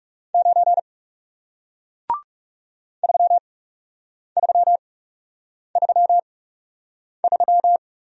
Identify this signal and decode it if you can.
Morse code, heard as a keyed tone, about 40 wpm, spaced as for 8 wpm, 700 Hz. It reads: 9 3333